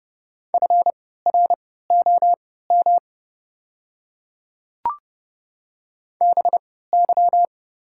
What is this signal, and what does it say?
Morse code, heard as a keyed tone, about 29 wpm, spaced as for 17 wpm, 700 Hz. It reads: FROM BY
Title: 3x Character Spacing - Top 100 Words - 30wpm